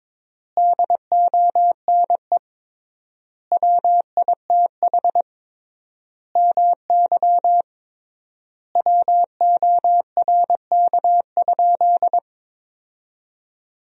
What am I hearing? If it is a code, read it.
Morse code, heard as a keyed tone, about 22 wpm, 700 Hz. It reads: DONE WITH MY WORK?
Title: 3x Word-Spacing - Sentences from Top 500 Words - 22wpm